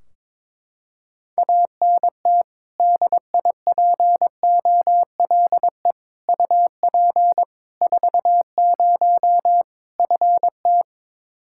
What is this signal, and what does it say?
Morse code, heard as a keyed tone, about 22 wpm, 700 Hz. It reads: ANT DIPOLE UP 40 FT
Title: Practice QSO - Exchange 2 - Info - 22wpm